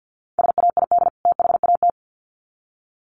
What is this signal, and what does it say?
Morse code, heard as a keyed tone, about 48 wpm, 700 Hz. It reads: SAID THAT
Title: Sets of 2 Words 50wpm